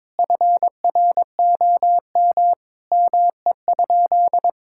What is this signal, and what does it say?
Morse code, heard as a keyed tone, about 22 wpm, 700 Hz. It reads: FROM ME?
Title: Sentences from Top 1600 Words 22wpm